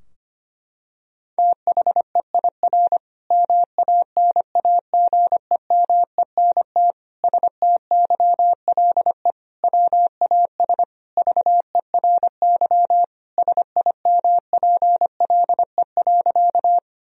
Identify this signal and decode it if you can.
Morse code, heard as a keyed tone, about 25 wpm, 700 Hz. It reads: THEIR MANAGEMENT STYLE WAS VERY SIMPLE.
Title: Sentences from Top 1000 Words - Review Part 4 - 25wpm